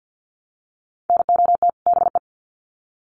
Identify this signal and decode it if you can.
Morse code, heard as a keyed tone, about 50 wpm, 700 Hz. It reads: NOT BE
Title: Sets of 2 Words 50wpm